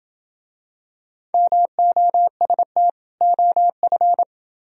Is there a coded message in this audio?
Yes: Morse code from 1.3 to 4.2 s, heard as a keyed tone, about 28 wpm, 700 Hz. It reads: MOST OF